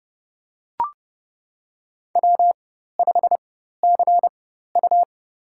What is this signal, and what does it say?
Morse code, heard as a keyed tone, about 30 wpm, spaced as for 14 wpm, 700 Hz. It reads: W5CU
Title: USA Call Signs - 4 Characters - 4x Character Spacing - 30wpm